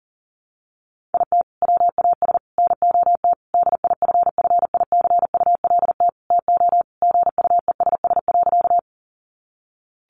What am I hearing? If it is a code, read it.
Morse code, heard as a keyed tone, about 37 wpm, 700 Hz. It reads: IT WAS NOT DIFFICULT TO GUESS.
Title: Sentences from Top 1500 Words 40wpm